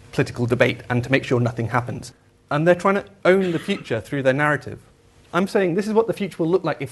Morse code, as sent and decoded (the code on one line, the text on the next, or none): none